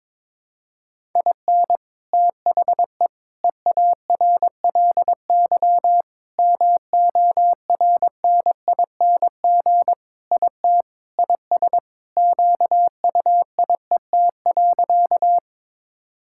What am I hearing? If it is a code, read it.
Morse code, heard as a keyed tone, about 21 wpm, 700 Hz. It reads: IN THE EARLY MORNING IT IS QUIET.
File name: practice / 22wpm / Sentences with 3-Word Phrases - 22wpm